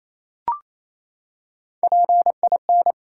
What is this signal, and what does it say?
Morse code, heard as a keyed tone, about 28 wpm, 700 Hz. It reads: PIN
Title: Three Letter Words - 28wpm